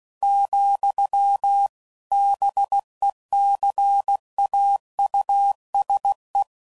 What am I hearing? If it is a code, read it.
Morse code, heard as a keyed tone, about 16 wpm, 800 Hz. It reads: , BECAUSE
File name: CW Code Groups @15WPM / Technical Subjects / AND Gate